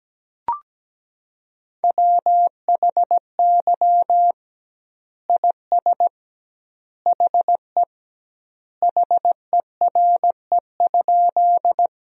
Morse code, heard as a keyed tone, about 17 wpm, 700 Hz. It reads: WHY IS HE HERE?
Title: 2x Word-Spacing - Sentences from Top 200 Words - 17wpm